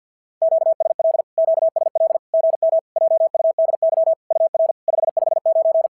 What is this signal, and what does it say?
Morse code, heard as a keyed tone, about 50 wpm, 650 Hz. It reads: QSL QSL GM JUDY UR 559
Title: SOTA QSO Practice 50wpm